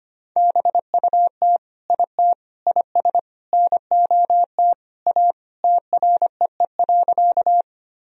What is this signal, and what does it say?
Morse code, heard as a keyed tone, about 25 wpm, 700 Hz. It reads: BUT IT IS NOT A TREE.
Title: Sentences from Top 1000 Words - Review Part 1 - 25wpm